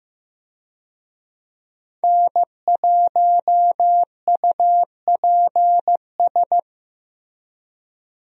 Morse code, heard as a keyed tone, about 15 wpm, 700 Hz. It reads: N1UPS
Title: USA Call Signs - 5 Characters - Repeated 3x - 15wpm